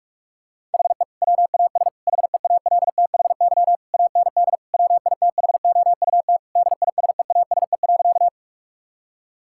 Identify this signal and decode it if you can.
Morse code, heard as a keyed tone, about 41 wpm, 700 Hz. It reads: HE WAS HEALTHY AND WITHOUT DISEASE.
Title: Sentences from Top 1000 Words - Review Part 6 - 45wpm